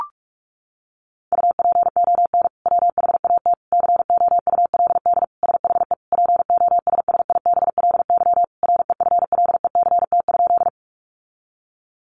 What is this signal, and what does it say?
Morse code, heard as a keyed tone, about 44 wpm, 700 Hz. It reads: UPON WHAT COULD SHE POSSIBLY REFLECT?
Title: Sentences from Top 1500 Words 45wpm